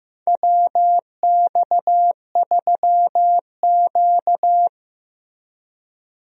Morse code, heard as a keyed tone, about 15 wpm, 700 Hz. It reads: WX3Q
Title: USA Call Signs - 4 Characters - Repeated 2x - 15wpm